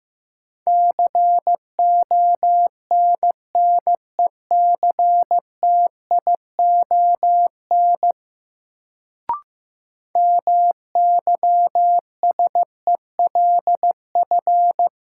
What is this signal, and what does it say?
Morse code, heard as a keyed tone, about 15 wpm, 700 Hz. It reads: CONNECTION MYSELF
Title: Sentences from Top 1300 Words - New Word Drill Alt - 15wpm